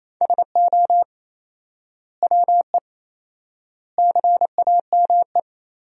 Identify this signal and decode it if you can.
Morse code, heard as a keyed tone, about 28 wpm, 700 Hz. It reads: SO WE CAME
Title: 4x Word-Spacing - Sentences from Top 400 Words - 28wpm